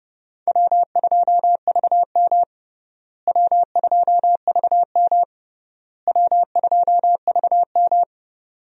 Morse code, heard as a keyed tone, about 30 wpm, 700 Hz. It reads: W2VM W2VM W2VM
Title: USA Call Signs - 4 Characters - Repeated 3x - 30wpm